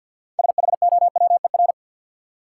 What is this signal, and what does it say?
Morse code, heard as a keyed tone, about 53 wpm, 700 Hz. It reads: SHOWER
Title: Six Letter Words - 50wpm